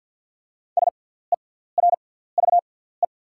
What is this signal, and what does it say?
Morse code, heard as a keyed tone, about 48 wpm, spaced as for 17 wpm, 690 Hz. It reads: SERVE
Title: 6x Character Spacing - Top 500 Words - 50wpm